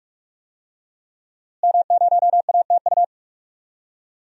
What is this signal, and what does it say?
Morse code, heard as a keyed tone, about 46 wpm, 700 Hz. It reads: M0ATU
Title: DXCC-100 Call Signs - 5 Characters - Repeated 3x - 45wpm